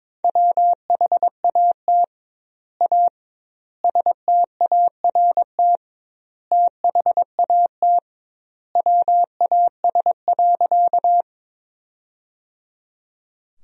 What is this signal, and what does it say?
Morse code, heard as a keyed tone, about 22 wpm, 700 Hz. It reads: WHAT A START THAT WAS.
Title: 2x Word-Spacing - Sentences from Top 300 Words - 22wpm